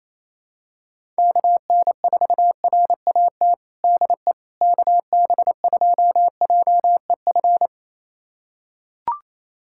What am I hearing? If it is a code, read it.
Morse code, heard as a keyed tone, about 27 wpm, 700 Hz. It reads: KN4RAT DE KB2JEF
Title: USA Callsign DE Callsign - 6 Characters - 28wpm